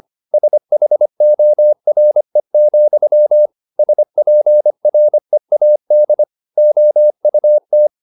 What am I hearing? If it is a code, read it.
Morse code, heard as a keyed tone, about 25 wpm, 600 Hz. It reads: SHORE, SPREAD OUT
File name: the old man and the sea part 1 25wpm